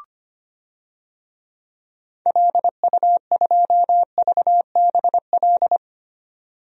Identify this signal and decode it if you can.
Morse code, heard as a keyed tone, about 25 wpm, 700 Hz. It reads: LU2VBL
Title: DXCC-100 Call Signs - 6 Characters - 25wpm